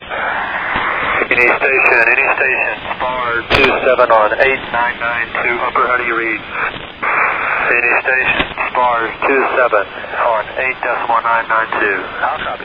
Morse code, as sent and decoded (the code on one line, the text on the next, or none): none